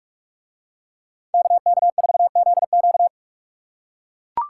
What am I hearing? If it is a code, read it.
Morse code, heard as a keyed tone, about 44 wpm, 700 Hz. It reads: KK4ZQ